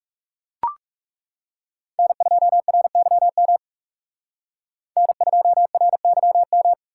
Morse code, heard as a keyed tone, about 38 wpm, 700 Hz. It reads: N2RYM N2RYM